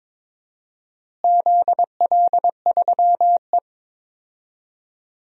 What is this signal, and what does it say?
Morse code, heard as a keyed tone, about 22 wpm, 700 Hz. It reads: ZL3E
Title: DXCC-100 Call Signs - 4 Characters - Repeated 3x - 22wpm